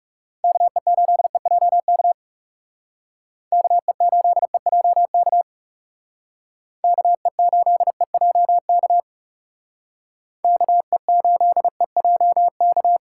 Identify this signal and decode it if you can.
Morse code, heard as a keyed tone, about 35 wpm, 700 Hz. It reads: KE8EJK KE8EJK KE8EJK KE8EJK
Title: Speed Racing - USA 2x3 Call Signs 30wpm